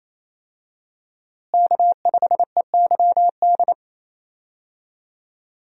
Morse code, heard as a keyed tone, about 28 wpm, 700 Hz. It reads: K5EYD
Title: USA Call Signs - 5 Characters - Repeated 2x - 28wpm